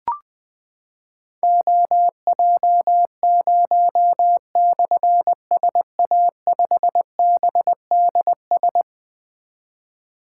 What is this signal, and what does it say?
Morse code, heard as a keyed tone, about 21 wpm, 700 Hz. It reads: OJ0/SA5BDS
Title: Difficult Call Signs 20wpm